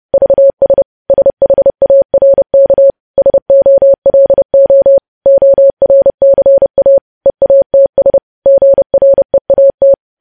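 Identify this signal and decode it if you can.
Morse code, heard as a keyed tone, about 30 wpm, 550 Hz. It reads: VS SHARK SOLO ORCA EATS GREAT
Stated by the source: Morse code practice of news headlines at 30 wpm, preceded by vvv.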